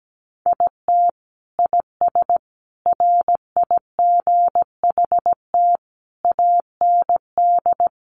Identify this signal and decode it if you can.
Morse code, heard as a keyed tone, about 17 wpm, 700 Hz. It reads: IT IS RIGHT AND
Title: Sentences from Top 1500 Words 17wpm